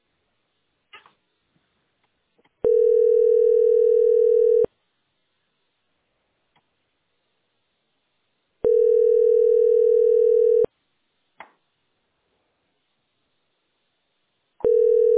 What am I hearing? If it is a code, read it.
Ringback tone (North America), 440 + 480 Hz.